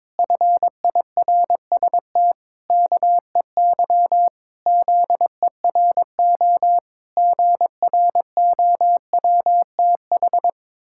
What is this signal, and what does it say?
Morse code, heard as a keyed tone, about 22 wpm, 700 Hz. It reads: FIRST KEY ZERO GROWTH
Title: Pangram Challenge v2 - 22wpm